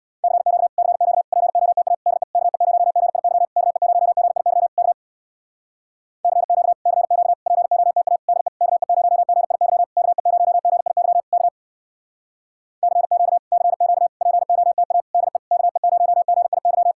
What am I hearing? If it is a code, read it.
Morse code, heard as a keyed tone, about 63 wpm, 700 Hz. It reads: CQ CQ POTA DE KE0GIY KE0GIY K CQ CQ POTA DE KE0GIY KE0GIY K CQ CQ POTA DE KE0GIY